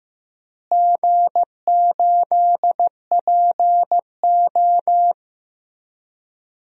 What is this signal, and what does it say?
Morse code, heard as a keyed tone, about 15 wpm, 700 Hz. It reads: G8PO
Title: DXCC-100 Call Signs - 4 Characters - Repeated 3x - 15wpm